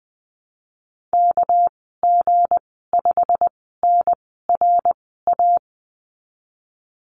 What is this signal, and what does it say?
Morse code, heard as a keyed tone, about 20 wpm, 700 Hz. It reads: K G 5 N R A
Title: 200 Lesson 32g - Letter k - Call Signs 20wpm